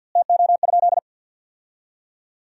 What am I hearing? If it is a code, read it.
Morse code, heard as a keyed tone, about 50 wpm, 700 Hz. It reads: TO?
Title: 2x Word-Spacing - Sentences from Top 100 Words - 50wpm